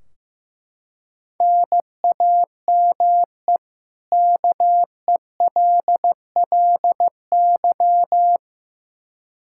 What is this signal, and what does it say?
Morse code, heard as a keyed tone, about 15 wpm, 700 Hz. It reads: NAME KELLY
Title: K1USN SST QSO Practice 15wpm